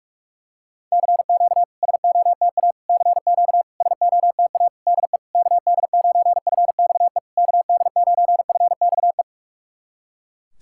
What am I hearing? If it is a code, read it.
Morse code, heard as a keyed tone, about 46 wpm, 690 Hz. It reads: CQ SOTA CQ SOTA DE KD9FXE KD9FXE
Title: SOTA QSO Practice 45wpm